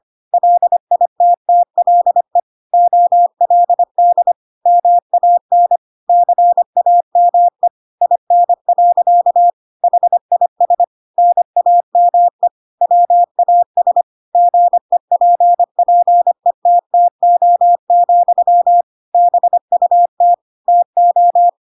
Morse code, heard as a keyed tone, about 25 wpm, 700 Hz. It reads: LITTLE OLD MAN CAME IN. HIS NAME WAS GEPPETTO, BUT TO